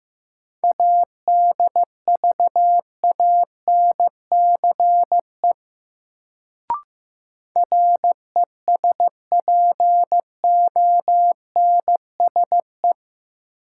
Morse code, heard as a keyed tone, about 15 wpm, 700 Hz. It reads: ADVANCE RESPONSE
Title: Sentences from Top 1400 Words - New Word Drill - 15wpm